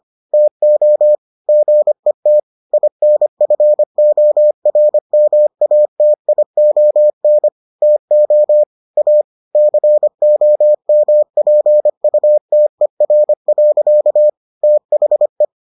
Morse code, heard as a keyed tone, about 25 wpm, 600 Hz. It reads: TO GET INFORMATION TO A COMPUTER. THE